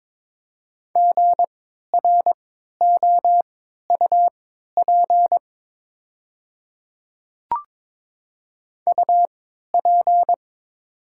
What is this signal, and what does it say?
Morse code, heard as a keyed tone, about 22 wpm, spaced as for 13 wpm, 700 Hz. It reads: GROUP UP
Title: 3x Character Spacing - Top 500 Words - 22wpm